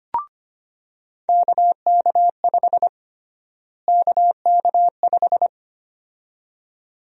Morse code, heard as a keyed tone, about 25 wpm, 700 Hz. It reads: KK5 KK5
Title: USA Call Sign Prefixes - Repeated 2x - 25wpm